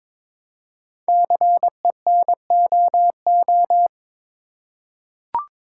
Morse code, heard as a keyed tone, about 22 wpm, 700 Hz. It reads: CENOO
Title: Code Groups - 5 Letters - 22wpm